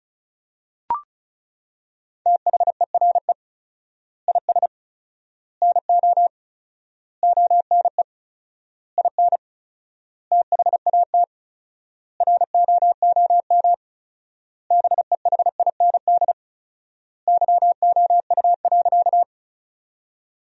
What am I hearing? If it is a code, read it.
Morse code, heard as a keyed tone, about 34 wpm, 700 Hz. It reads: THERE IS NO ONE IN THAT ROOM BEHIND YOU.
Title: 4x Word-Spacing - Sentences from Top 500 Words - 35wpm